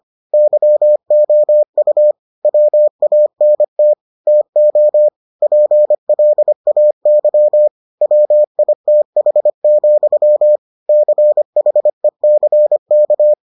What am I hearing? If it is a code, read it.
Morse code, heard as a keyed tone, about 25 wpm, 600 Hz. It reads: YOU WANT TO PLAY WITH, CHECK